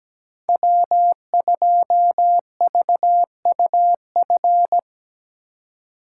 Morse code, heard as a keyed tone, about 17 wpm, 700 Hz. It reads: W2VUF
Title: K1USN SST QSO Practice - 17wpm